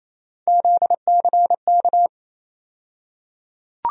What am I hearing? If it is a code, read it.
Morse code, heard as a keyed tone, about 28 wpm, 700 Hz. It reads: ZCK
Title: USA Call Sign Postfixes - 3 Letters 28wpm